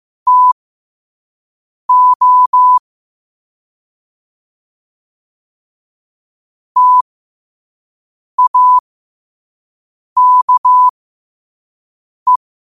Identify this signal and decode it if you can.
Morse code, heard as a keyed tone, about 15 wpm, spaced as for 5 wpm, 1000 Hz. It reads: TO TAKE
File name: qotdfiles / 5wpm